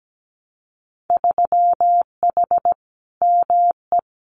Morse code, heard as a keyed tone, about 17 wpm, 700 Hz. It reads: 3H ME
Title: Winter Field Day Exchanges Encore 17wpm